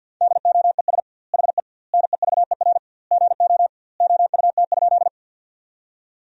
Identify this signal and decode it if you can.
Morse code, heard as a keyed tone, about 48 wpm, 700 Hz. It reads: DOES HE NEVER GO OUT?
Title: 2x Word-Spacing - Sentences from Top 300 Words - 50wpm